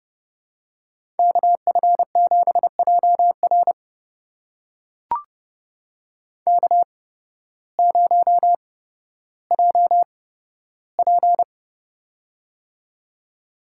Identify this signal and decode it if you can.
Morse code, heard as a keyed tone, about 30 wpm, spaced as for 14 wpm, 700 Hz. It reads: KF7JR K0JP
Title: USA Call Signs - 8x Character Spacing - 30wpm